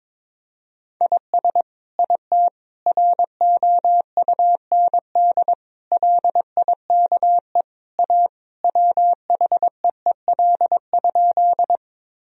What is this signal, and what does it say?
Morse code, heard as a keyed tone, about 22 wpm, 700 Hz. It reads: IS IT ROUND LIKE A WHEEL?
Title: Sentences from Top 1000 Words - Review Part 3 - 22wpm